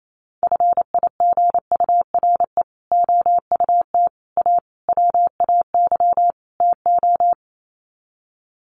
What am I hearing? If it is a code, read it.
Morse code, heard as a keyed tone, about 28 wpm, 700 Hz. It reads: FIGURE OUT A WAY TO
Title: Sets of 5 Words Encore2 28wpm